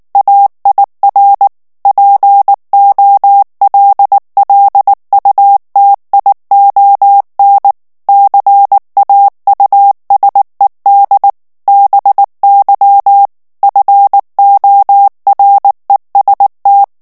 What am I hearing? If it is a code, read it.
Morse code, heard as a keyed tone, about 19 wpm, 800 Hz. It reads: AIR POLLUTION CAUSED BY FOREST
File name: ANGLAIS texte clair 2 vitesse 18 mots minute